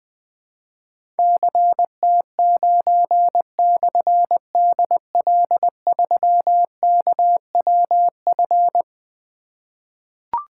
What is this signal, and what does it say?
Morse code, heard as a keyed tone, about 20 wpm, 700 Hz. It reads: CT9/DL3KWF
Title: Difficult Call Signs 20wpm